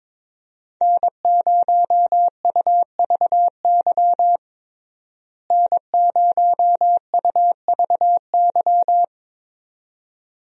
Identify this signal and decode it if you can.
Morse code, heard as a keyed tone, about 22 wpm, 700 Hz. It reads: N0UVY N0UVY
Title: USA Call Signs - 5 Characters - Repeated 2x - 22wpm